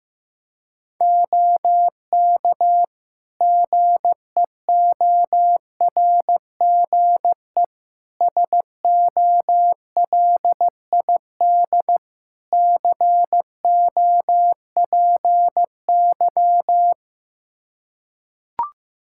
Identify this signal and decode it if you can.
Morse code, heard as a keyed tone, about 15 wpm, 700 Hz. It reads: OK GEORGE SOLID COPY
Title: K1USN SST QSO Practice 15wpm